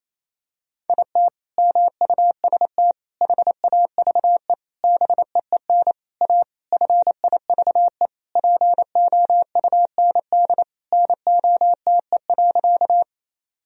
Morse code, heard as a keyed tone, about 28 wpm, 700 Hz. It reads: IT MUST HAVE BEEN A FIVE POUND NOTE.
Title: Sentences from Top 1000 Words - Review Part 2 - 28wpm